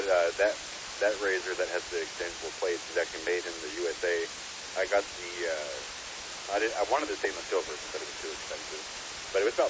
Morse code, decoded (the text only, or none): none